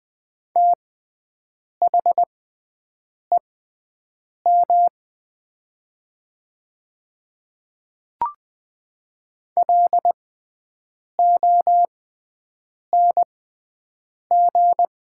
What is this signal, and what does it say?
Morse code, heard as a keyed tone, about 20 wpm, spaced as for 7 wpm, 700 Hz. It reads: THEM LONG
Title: 6x Character Spacing - Top 100 Words - 20wpm